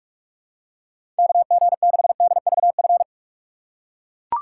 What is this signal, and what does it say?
Morse code, heard as a keyed tone, about 42 wpm, 700 Hz. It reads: KG6DVF
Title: USA Call Signs - 3x Character Spacing - 45wpm